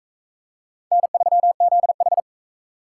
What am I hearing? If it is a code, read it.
Morse code, heard as a keyed tone, about 43 wpm, 700 Hz. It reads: N3ZH